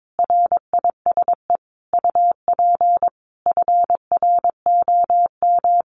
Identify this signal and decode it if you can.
Morse code, heard as a keyed tone, about 22 wpm, 700 Hz. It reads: RISE UP FROM
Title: Sentences from Top 900 Words 22wpm